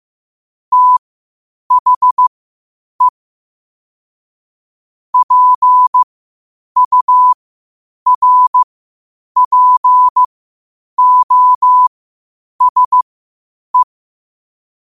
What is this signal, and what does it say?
Morse code, heard as a keyed tone, about 15 wpm, spaced as for 8 wpm, 1000 Hz. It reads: THE PURPOSE
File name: qotdfiles / 8wpm